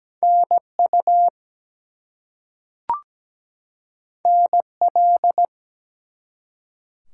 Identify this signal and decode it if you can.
Morse code, heard as a keyed tone, about 17 wpm, 700 Hz. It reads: NU NL
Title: US States and Canadian Provinces 17wpm